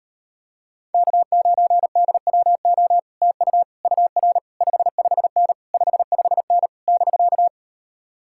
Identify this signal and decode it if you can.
Morse code, heard as a keyed tone, about 38 wpm, 700 Hz. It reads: K9DWO TU UR 55N 55N <BK>